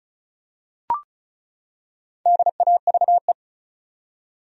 Morse code, heard as a keyed tone, about 36 wpm, 700 Hz. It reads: DAVE
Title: USA Common Names - 35wpm